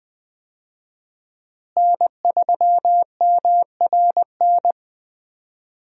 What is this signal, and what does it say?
Morse code, heard as a keyed tone, about 21 wpm, 700 Hz. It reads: N3MRN
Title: USA Call Signs - 5 Characters - Repeated 3x - 20wpm